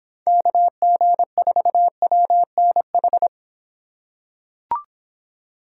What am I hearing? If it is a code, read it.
Morse code, heard as a keyed tone, about 26 wpm, 700 Hz. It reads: KG4WNH